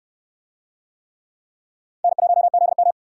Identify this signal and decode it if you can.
Morse code, heard as a keyed tone, about 69 wpm, 700 Hz. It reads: N1ZM